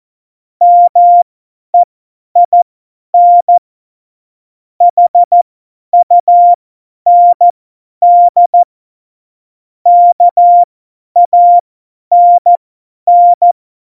Morse code, heard as a keyed tone, about 14 wpm, spaced as for 10 wpm, 700 Hz. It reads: MEIN HUND KANN
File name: Morsekurs / Witze / Witz 25. Dezember, 50 F70